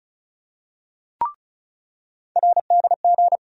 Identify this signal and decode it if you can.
Morse code, heard as a keyed tone, about 35 wpm, 700 Hz. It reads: RDG
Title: USA Call Sign Postfixes - 3 Letters 35wpm